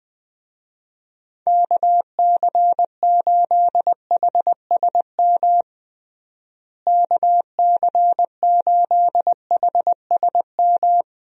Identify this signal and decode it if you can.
Morse code, heard as a keyed tone, about 20 wpm, 700 Hz. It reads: KC8HSM KC8HSM
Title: USA Call Signs - 6 Characters - Repeated 2x - 20wpm